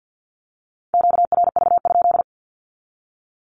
Morse code, heard as a keyed tone, about 50 wpm, 700 Hz. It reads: QRV?
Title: Q-Codes with Spoken Meaning - 50wpm